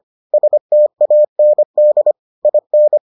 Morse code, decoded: STAND IN

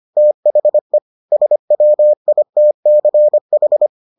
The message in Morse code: THE SWITCH

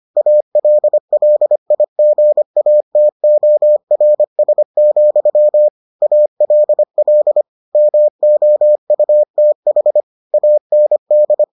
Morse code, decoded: ALLIGATORS, ALL MOUTH AND

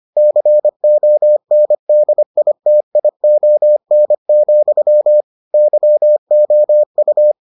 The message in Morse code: CONDITION, YOU

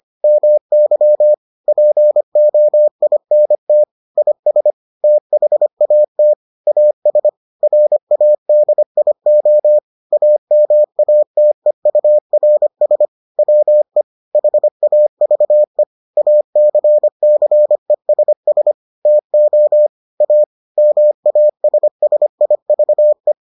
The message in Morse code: MY POINT IS THAT AS RADIO AMATEURS WE HAVE ACCESS TO A MASSIVE

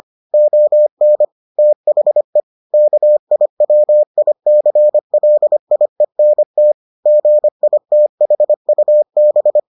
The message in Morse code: ON THE KIWICLIENT GITHUB

